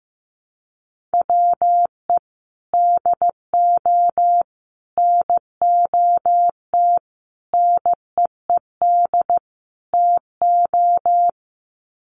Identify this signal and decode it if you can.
Morse code, heard as a keyed tone, about 15 wpm, 700 Hz. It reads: WE DO NOT NEED TO
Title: Sets of 5 Words Encore 15wpm